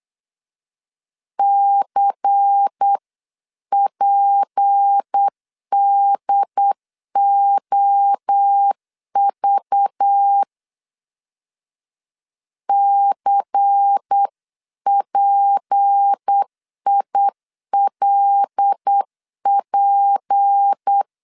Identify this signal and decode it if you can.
Morse code, heard as a keyed tone, about 8 wpm, 800 Hz. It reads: CPDOV CPILP
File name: LEZIONI / 6 PKX0 / 6 P 8wpm 3025